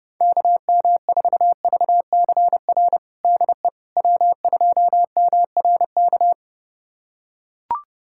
Morse code, heard as a keyed tone, about 30 wpm, 700 Hz. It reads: KM4VCR DE W2MRK
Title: USA Callsign DE Callsign 30wpm